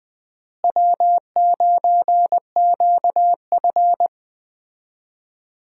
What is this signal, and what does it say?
Morse code, heard as a keyed tone, about 20 wpm, 700 Hz. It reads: W9QF
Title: USA Call Signs - 4 Characters - Repeated 3x - 20wpm